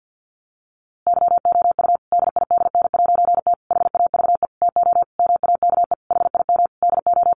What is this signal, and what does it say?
Morse code, heard as a keyed tone, about 50 wpm, 700 Hz. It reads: YOU DIDN'T HAVE TO MAKE HIM DO
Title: A Sound Of Thunder 50wpm